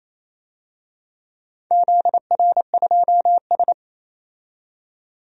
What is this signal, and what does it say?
Morse code, heard as a keyed tone, about 28 wpm, 700 Hz. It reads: ZR2S
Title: DXCC-100 Call Signs - 4 Characters - 28wpm